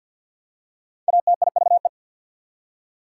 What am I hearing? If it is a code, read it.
Morse code, heard as a keyed tone, about 48 wpm, 690 Hz. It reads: ATIVE
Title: Prefixes and Suffixes 50wpm